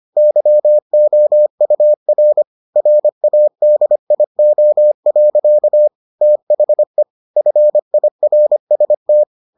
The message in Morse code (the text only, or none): YOUR RADIO. THE FIRST